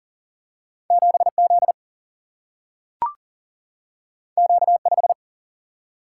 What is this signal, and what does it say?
Morse code, heard as a keyed tone, about 40 wpm, 700 Hz. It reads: 7 Z Q 5
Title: Single-Letter Single-Number 40wpm